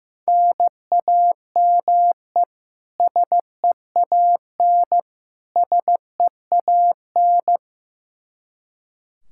Morse code, heard as a keyed tone, about 15 wpm, 700 Hz. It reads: NAME SEAN SEAN
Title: Practice QSO - Exchange 1 - Beginning - 15wpm